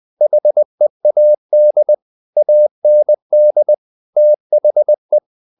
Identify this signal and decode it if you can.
Morse code, heard as a keyed tone, about 20 wpm, 600 Hz. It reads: HEAD AND THE
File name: the old man and the sea part 2 20wpm